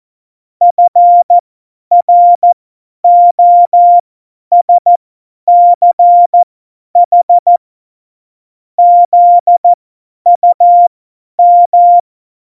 Morse code, heard as a keyed tone, about 14 wpm, spaced as for 10 wpm, 700 Hz. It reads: FROSCH ZUM
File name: Morsekurs / Witze / Witz 14. August 2018, 50 F70